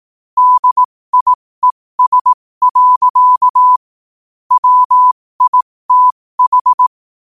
Morse code, heard as a keyed tone, about 18 wpm, spaced as for 15 wpm, 1000 Hz. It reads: DIES. WITH
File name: qotdfiles / 15wpm